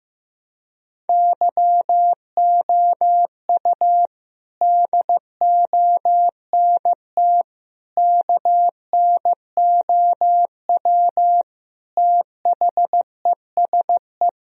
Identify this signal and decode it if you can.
Morse code, heard as a keyed tone, about 15 wpm, 700 Hz. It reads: YOU DONT KNOW THESE
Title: Sentences from Top 1000 Words - Review Part 1 - 15wpm